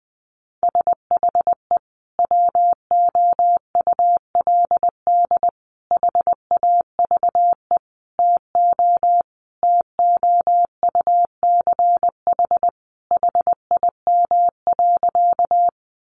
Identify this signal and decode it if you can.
Morse code, heard as a keyed tone, about 20 wpm, 700 Hz. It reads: SHE WOULD HAVE TO TOUCH HIM.
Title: Sentences from Top 1200 Words 20wpm